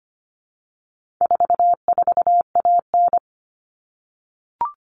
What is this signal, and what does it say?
Morse code, heard as a keyed tone, about 25 wpm, 700 Hz. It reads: 44AN
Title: All Cut Numbers - 4-Digit 25wpm